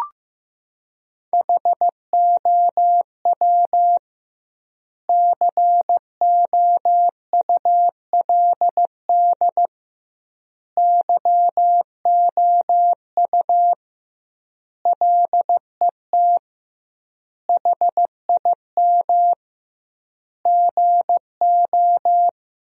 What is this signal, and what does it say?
Morse code, heard as a keyed tone, about 15 wpm, 700 Hz. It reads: HOW COULD YOU LET HIM GO
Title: 2x Word-Spacing - Sentences from Top 300 Words - 15wpm